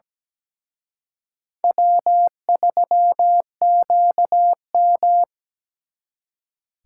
Morse code, heard as a keyed tone, about 20 wpm, 700 Hz. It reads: W3QM
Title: Speed Racing - USA Call Signs 17wpm